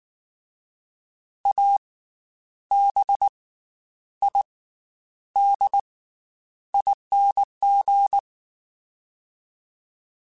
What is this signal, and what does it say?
Morse code, heard as a keyed tone, about 19 wpm, 800 Hz. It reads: A B I D ING
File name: Special INGs / Special INGs-mp3s / Slow ing @18